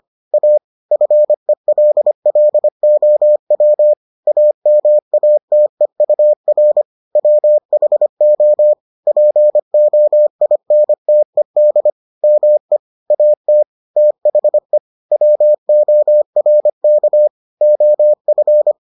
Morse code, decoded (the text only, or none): A FELLOW AMATEUR WHO POINTED ME AT THE WORK OF